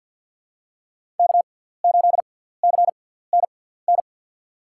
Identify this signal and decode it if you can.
Morse code, heard as a keyed tone, about 47 wpm, spaced as for 17 wpm, 700 Hz. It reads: K8CNN
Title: USA Call Signs - 6x Character Spacing - 50wpm